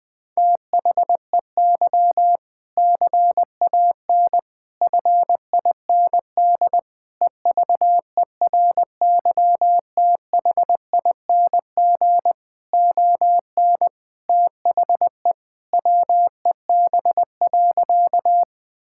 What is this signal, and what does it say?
Morse code, heard as a keyed tone, about 20 wpm, 700 Hz. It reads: THEY CAN FIND EVERYTHING ON THE WEB.
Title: All-Age Appropriate Jokes - 20wpm